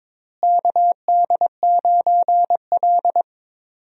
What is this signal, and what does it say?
Morse code, heard as a keyed tone, about 22 wpm, 700 Hz. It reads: KD9L
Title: USA Call Signs - 4 Characters - Repeated 3x - 22wpm